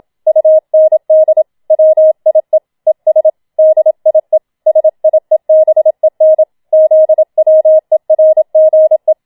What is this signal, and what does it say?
Morse code, heard as a keyed tone, about 27 wpm, 630 Hz. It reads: UND WIE ES DIE SIEBEN ZWERGE
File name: Schneewittchen Teil2 27wpm